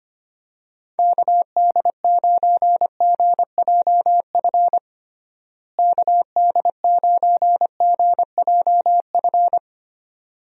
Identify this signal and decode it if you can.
Morse code, heard as a keyed tone, about 25 wpm, 700 Hz. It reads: KD9GJF KD9GJF
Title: USA Call Signs - 6 Characters - Repeated 2x - 25wpm